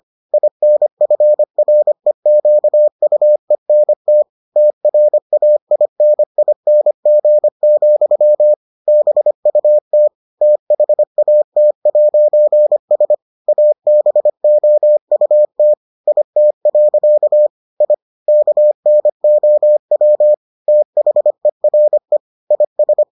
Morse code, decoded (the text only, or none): INFREQUENT TRAINING, BUT THAT'S ABOUT IT. I KNOW THERE IS